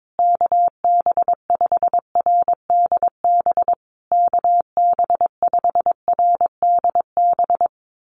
Morse code, heard as a keyed tone, about 22 wpm, 700 Hz. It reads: KB5RDB KB5RDB
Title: USA Call Signs Repeated 3x 22wpm